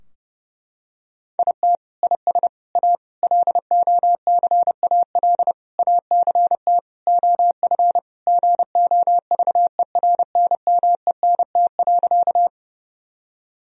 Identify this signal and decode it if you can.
Morse code, heard as a keyed tone, about 30 wpm, 700 Hz. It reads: IT IS A LOCAL ACT OF GOVERNMENT.
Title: Sentences from Top 1000 Words - Review Part 3 - 30wpm